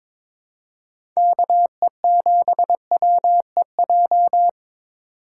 Morse code, heard as a keyed tone, about 21 wpm, 700 Hz. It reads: KE7WEJ
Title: USA Call Signs - 6 Characters - 4x Character Spacing - 22wpm